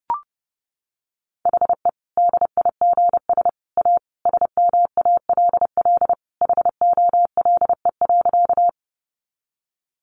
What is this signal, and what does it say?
Morse code, heard as a keyed tone, about 30 wpm, 700 Hz. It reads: HE DIGS A SMALL HOLE.
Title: Easy Phrases 30wpm